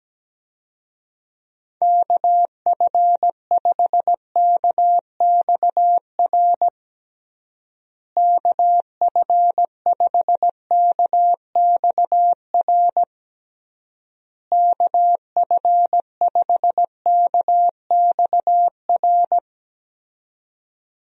Morse code, heard as a keyed tone, about 17 wpm, 700 Hz. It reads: KF5KXR KF5KXR KF5KXR
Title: USA Call Signs - 6 Characters - Repeated 3x - 17wpm